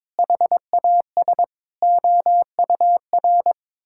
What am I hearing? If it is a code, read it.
Morse code, heard as a keyed tone, about 22 wpm, 700 Hz. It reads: HAS OUR
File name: Sentences with 9-Word Phrases - 22wpm